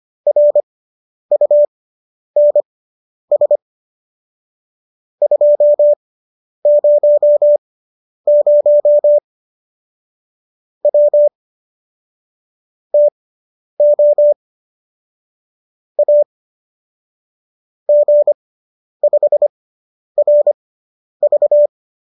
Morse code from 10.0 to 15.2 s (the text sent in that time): W TO